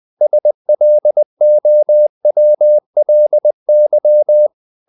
Morse code, heard as a keyed tone, about 20 wpm, 600 Hz. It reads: SLOWLY
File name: the old man and the sea / the old man and the sea part 1 20wpm